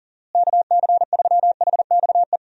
Morse code, heard as a keyed tone, about 40 wpm, 700 Hz. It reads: KC3HXE
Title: USA Call Signs - 6 Characters - Repeated 3x - 40wpm